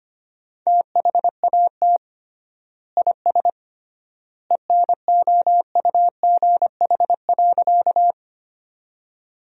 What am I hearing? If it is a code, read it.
Morse code, heard as a keyed tone, about 25 wpm, 700 Hz. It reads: THAT IS ENOUGH.
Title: 3x Word-Spacing - Sentences from Top 400 Words - 25wpm